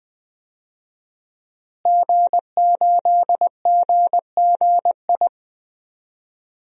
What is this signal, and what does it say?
Morse code, heard as a keyed tone, about 20 wpm, 700 Hz. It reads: G8GGI
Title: DXCC-100 Call Signs - 5 Characters - Repeated 2x - 20wpm